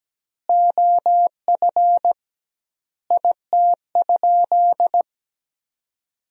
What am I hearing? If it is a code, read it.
Morse code, heard as a keyed tone, about 17 wpm, 700 Hz. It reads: OF IT?
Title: 2x Word-Spacing - Sentences from Top 300 Words - 17wpm